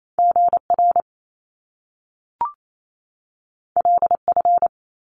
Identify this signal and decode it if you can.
Morse code, heard as a keyed tone, about 28 wpm, 700 Hz. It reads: GR LF
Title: Top Two Letter Combinations in Words 28wpm